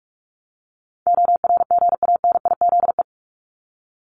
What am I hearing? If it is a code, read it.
Morse code, heard as a keyed tone, about 44 wpm, 700 Hz. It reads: ORGANIZE
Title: Top 1751-2000 Words 45wpm